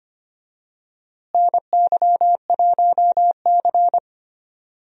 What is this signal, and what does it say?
Morse code, heard as a keyed tone, about 25 wpm, 700 Hz. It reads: NY1C
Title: USA Call Signs - 4 Characters - Repeated 2x - 25wpm